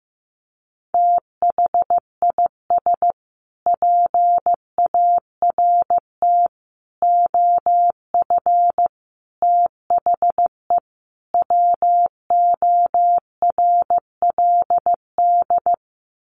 Morse code, heard as a keyed tone, about 14 wpm, 700 Hz. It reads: THIS PART OF THE WORLD
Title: Sets of 5 Words Encore2 15wpm